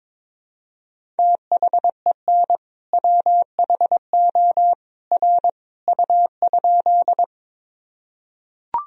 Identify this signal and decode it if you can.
Morse code, heard as a keyed tone, about 22 wpm, 700 Hz. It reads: THEN WHO R U?
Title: Sentences from Top 100 Words and Abbreviations 22wpm